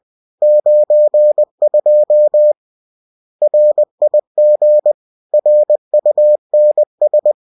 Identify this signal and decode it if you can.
Morse code, heard as a keyed tone, about 20 wpm, 600 Hz. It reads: 92 RIG RUNS